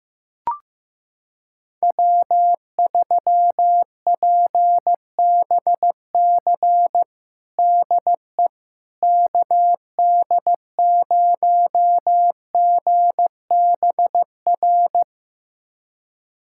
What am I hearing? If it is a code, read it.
Morse code, heard as a keyed tone, about 15 wpm, 700 Hz. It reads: W3PBC DE KD0GBR
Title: USA Callsign DE Callsign 15wpm